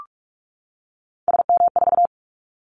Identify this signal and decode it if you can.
Morse code, heard as a keyed tone, about 46 wpm, 700 Hz. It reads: SM4
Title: Callsign Prefixes for Top 25 DXCC - 45wpm